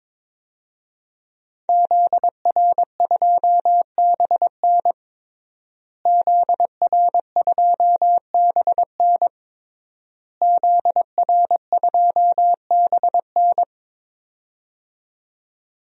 Morse code, heard as a keyed tone, about 22 wpm, 700 Hz. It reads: ZR2BN ZR2BN ZR2BN
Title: DXCC-100 Call Signs - 5 Characters - Repeated 3x - 22wpm